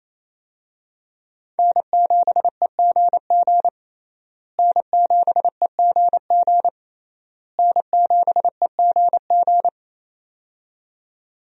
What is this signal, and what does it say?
Morse code, heard as a keyed tone, about 27 wpm, 700 Hz. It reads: N7EGG N7EGG N7EGG
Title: USA Call Signs - 5 Characters - Repeated 3x - 28wpm